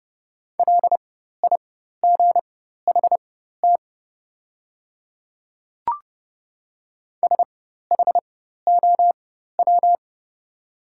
Morse code, heard as a keyed tone, about 30 wpm, spaced as for 14 wpm, 700 Hz. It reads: LIGHT SHOW
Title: 4x Character Spacing - Top 200 Words - 30wpm